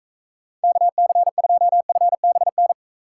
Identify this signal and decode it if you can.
Morse code, heard as a keyed tone, about 43 wpm, 700 Hz. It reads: KC2FBN